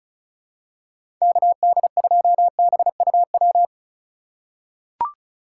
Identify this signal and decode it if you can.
Morse code, heard as a keyed tone, about 36 wpm, 700 Hz. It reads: KD2BUW
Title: USA Call Signs - 6 Characters - 4x Character Spacing - 35wpm